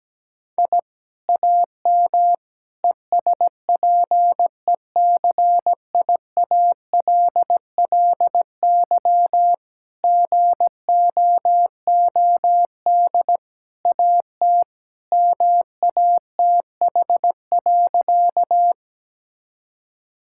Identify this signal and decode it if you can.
Morse code, heard as a keyed tone, about 17 wpm, 700 Hz. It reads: I AM ESPECIALLY GOOD AT MATH.
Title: Sentences from Top 1000 Words - Review Part 6 - 17wpm